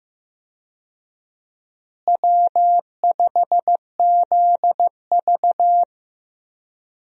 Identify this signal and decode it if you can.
Morse code, heard as a keyed tone, about 15 wpm, 700 Hz. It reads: W5ZV
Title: USA Call Signs - 4 Characters - Repeated 2x - 15wpm